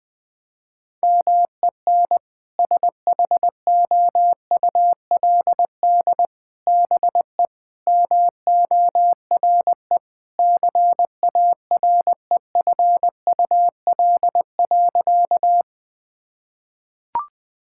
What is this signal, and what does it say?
Morse code, heard as a keyed tone, about 20 wpm, 700 Hz. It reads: MEN SHOULD BE MORE CAREFUL.
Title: Sentences from Top 1600 Words 20wpm